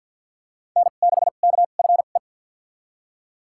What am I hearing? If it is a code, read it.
Morse code, heard as a keyed tone, about 51 wpm, spaced as for 36 wpm, 700 Hz. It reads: N6KFE
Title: USA Call Signs - 5 Characters - 2x Character Spacing - 50wpm